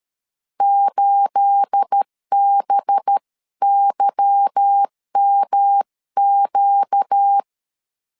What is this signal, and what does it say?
Morse code, heard as a keyed tone, about 13 wpm, 800 Hz. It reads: 8BYMQ
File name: LEZIONI / 10 SIGNS / 10C DOT 12wpm 2451